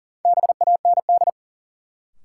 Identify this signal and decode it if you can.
Morse code, heard as a keyed tone, about 40 wpm, 700 Hz. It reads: BAND
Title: Sentences from Top 1600 Words - New Word Drill - 40wpm